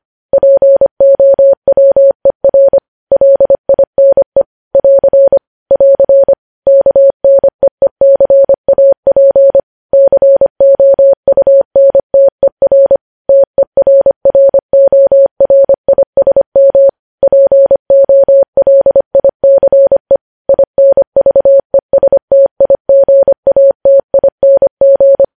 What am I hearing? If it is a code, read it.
Morse code, heard as a keyed tone, about 25 wpm, 550 Hz. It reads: POWER LINE <AR> <AR> KNEECAP COUNTER TERRORISM POLICE INVESTIGATING
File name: podcast / News headlines-1x-25x25